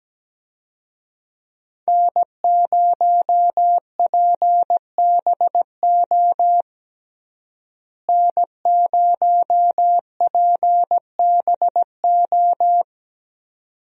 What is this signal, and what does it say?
Morse code, heard as a keyed tone, about 17 wpm, 700 Hz. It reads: N0PBO N0PBO
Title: USA Call Signs - 5 Characters - Repeated 2x - 17wpm